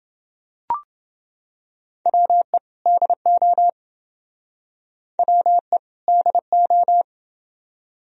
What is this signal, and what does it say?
Morse code, heard as a keyed tone, about 27 wpm, 700 Hz. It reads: WE DO WE DO